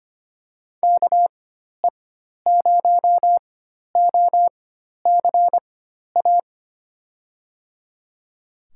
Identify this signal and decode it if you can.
Morse code, heard as a keyed tone, about 25 wpm, spaced as for 12 wpm, 700 Hz. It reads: KE0OCA
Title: USA Call Signs - 6 Characters - 4x Character Spacing - 25wpm